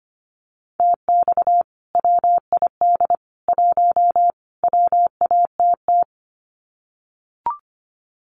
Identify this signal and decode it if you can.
Morse code, heard as a keyed tone, about 25 wpm, 700 Hz. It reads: TX WID 1 WATT
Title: Common QSO Phrases 25wpm